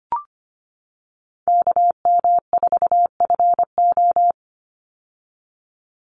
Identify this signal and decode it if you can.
Morse code, heard as a keyed tone, about 25 wpm, 700 Hz. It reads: KM4FO
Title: Call Signs - 25wpm